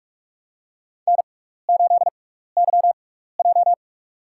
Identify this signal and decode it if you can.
Morse code, heard as a keyed tone, about 42 wpm, spaced as for 15 wpm, 700 Hz. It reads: N8YJ
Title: USA Call Signs - 4 Characters - 6x Character Spacing - 45wpm